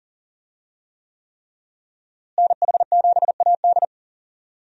Morse code, heard as a keyed tone, about 40 wpm, 700 Hz. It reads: NH7AD
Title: USA Call Signs - 5 Characters - Repeated 3x - 40wpm